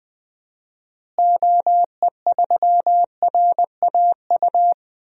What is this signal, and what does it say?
Morse code, heard as a keyed tone, about 20 wpm, 700 Hz. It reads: OE3RAU
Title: DXCC-100 Call Signs - 6 Characters - Repeated 3x - 20wpm